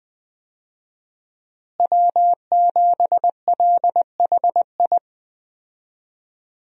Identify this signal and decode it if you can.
Morse code, heard as a keyed tone, about 20 wpm, 700 Hz. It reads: W7LHI
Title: USA Call Signs - 5 Characters - Repeated 3x - 20wpm